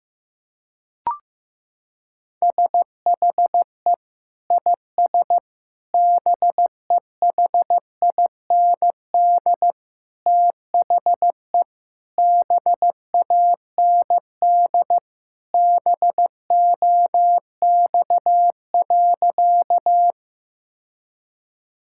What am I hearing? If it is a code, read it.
Morse code, heard as a keyed tone, about 15 wpm, 700 Hz. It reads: SHE IS BEHIND THE BAND BOX.
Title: Sentences from Top 1600 Words 15wpm